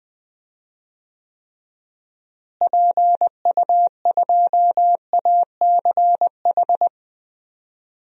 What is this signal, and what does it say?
Morse code, heard as a keyed tone, about 20 wpm, 700 Hz. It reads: PU2ACH
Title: DXCC-100 Call Signs - 6 Characters - Repeated 3x - 20wpm